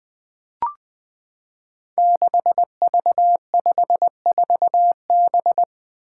Morse code, heard as a keyed tone, about 20 wpm, 700 Hz. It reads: 6V54B
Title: Five Challenging Letter-Number Combinations 20wpm